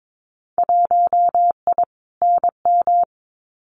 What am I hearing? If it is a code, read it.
Morse code, heard as a keyed tone, about 22 wpm, 700 Hz. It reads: 1I NM
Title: Winter Field Day Exchanges Encore 22wpm